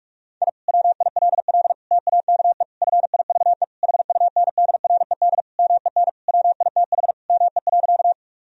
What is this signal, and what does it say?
Morse code, heard as a keyed tone, about 41 wpm, 700 Hz. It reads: I WILL TAKE FIVE HUNDRED MEN WITH ME.
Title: Sentences from Top 1000 Words - Review Part 2 - 45wpm